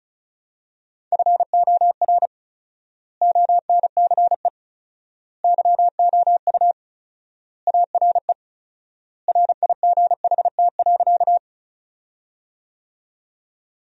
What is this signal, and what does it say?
Morse code, heard as a keyed tone, about 36 wpm, 700 Hz. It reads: FOR ONCE YOU ARE RIGHT.
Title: 4x Word-Spacing - Sentences from Top 300 Words - 35wpm